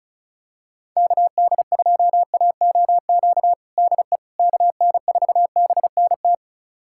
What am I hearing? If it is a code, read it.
Morse code, heard as a keyed tone, about 34 wpm, 700 Hz. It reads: KD2AOQ DE KN4BNT
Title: USA Callsign DE Callsign - 6 Characters - 35wpm